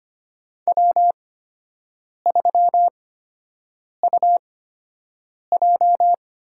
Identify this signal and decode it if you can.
Morse code, heard as a keyed tone, about 25 wpm, spaced as for 7 wpm, 700 Hz. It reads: W3UJ
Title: USA Call Signs - 4 Characters - 8x Character Spacing - 25wpm